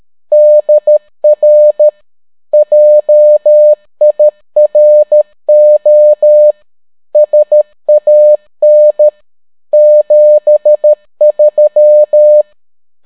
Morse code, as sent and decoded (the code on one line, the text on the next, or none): -.. .-. / .--- .. .-. --- / ... .- -. / --... ...--
DR JIRO SAN 73